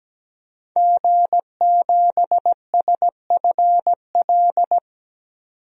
Morse code, heard as a keyed tone, about 17 wpm, 700 Hz. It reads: G7SFL
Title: DXCC-100 Call Signs - 5 Characters - 17wpm